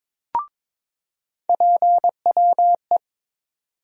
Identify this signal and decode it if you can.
Morse code, heard as a keyed tone, about 22 wpm, 700 Hz. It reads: PWE
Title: USA Call Sign Postfixes - 3 Letters 22wpm